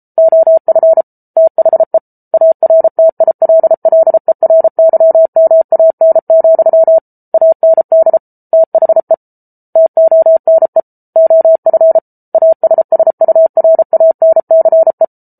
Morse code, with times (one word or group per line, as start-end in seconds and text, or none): 0.2-1.0 OF
1.4-2.0 THE
2.3-7.0 ARTILLERYMAN,
7.3-8.2 AND
8.5-9.1 THE
9.8-10.8 TONE
11.2-12.0 OF
12.3-15.0 ASSURANCE